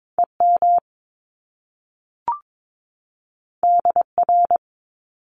Morse code, heard as a keyed tone, about 22 wpm, 700 Hz. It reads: EM DR
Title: Top Two Letter Combinations in Words 22wpm